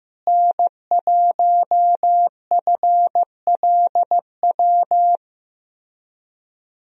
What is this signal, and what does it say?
Morse code, heard as a keyed tone, about 16 wpm, 700 Hz. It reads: N1FLW
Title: USA Call Signs - 5 Characters - Repeated 3x - 15wpm